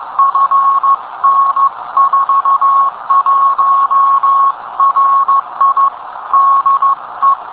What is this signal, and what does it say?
Morse code, heard as a keyed tone, about 15 wpm, 1100 Hz. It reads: FN41RI DE